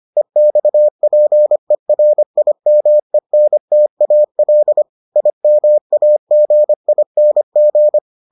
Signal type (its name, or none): Morse code